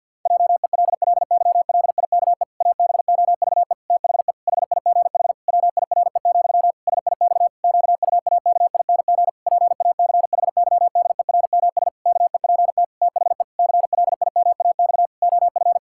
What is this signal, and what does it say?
Morse code, heard as a keyed tone, about 52 wpm, 700 Hz. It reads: JELLYLIKE ABOVE THE HIGH WIRE, SIX QUAKING PACHYDERMS KEPT THE CLIMAX OF
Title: Pangram Challenge v2 - 50wpm